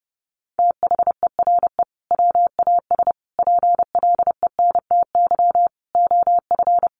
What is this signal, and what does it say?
Morse code, heard as a keyed tone, about 30 wpm, 700 Hz. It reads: THERE WAS PLENTY OF
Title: Sentences from Top 1300 Words 30wpm